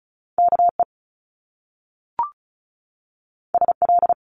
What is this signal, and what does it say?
Morse code, heard as a keyed tone, about 36 wpm, 700 Hz. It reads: KE SL
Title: Top Two Letter Combinations in Words 35wpm